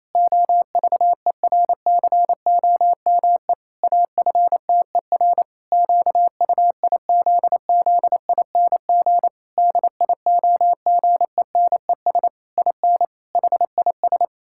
Morse code, heard as a keyed tone, about 28 wpm, 700 Hz. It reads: OVERCOME AFTER QUIZZING DIOGENES IN HIS
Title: Pangram Challenge v2 - 28wpm